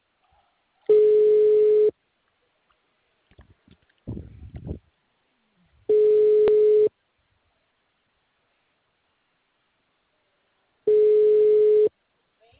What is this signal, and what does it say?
Ringback tone (Europe, 425 Hz).
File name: out- -2023